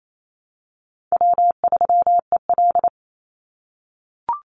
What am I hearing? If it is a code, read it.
Morse code, heard as a keyed tone, about 28 wpm, 700 Hz. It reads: W3EL
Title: Easy Call Signs - 28wpm